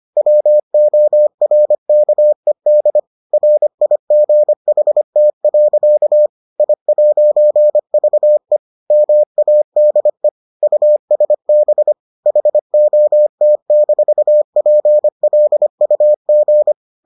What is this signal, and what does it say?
Morse code, heard as a keyed tone, about 25 wpm, 600 Hz. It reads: WORKED RIGHT. I'VE MADE USB HOT-PLUG